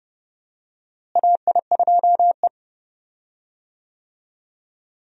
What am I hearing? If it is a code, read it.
Morse code, heard as a keyed tone, about 30 wpm, 700 Hz. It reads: AI2E
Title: USA Call Signs - 4 Characters - Repeated 3x - 30wpm